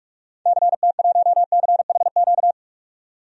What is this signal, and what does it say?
Morse code, heard as a keyed tone, about 47 wpm, 700 Hz. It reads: CT1CHQ
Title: DXCC-100 Call Signs - 6 Characters - 45wpm